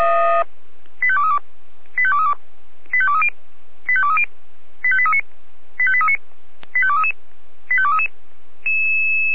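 Five-tone selective call (ZVEI), heard as quick runs of five tones, tones of about 80 ms. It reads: 85232 85232 85230 85230 85839 85839 85233 85233
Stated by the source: ZVEI five-tone call 85830 received via SDR.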